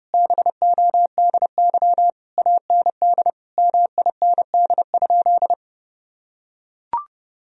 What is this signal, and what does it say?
Morse code, heard as a keyed tone, about 30 wpm, 700 Hz. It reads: BODY AND MIND?
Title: Sentences from Top 1000 Words - Review Part 2 - 30wpm